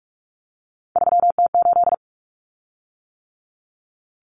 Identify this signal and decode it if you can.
Morse code, heard as a keyed tone, about 44 wpm, 700 Hz. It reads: 3T8
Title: Common Cut Numbers - 3-Digit 45wpm